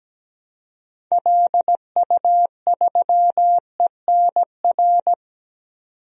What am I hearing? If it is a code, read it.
Morse code, heard as a keyed tone, about 17 wpm, 700 Hz. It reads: LU3ENR
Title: DXCC-100 Call Signs - 6 Characters - 17wpm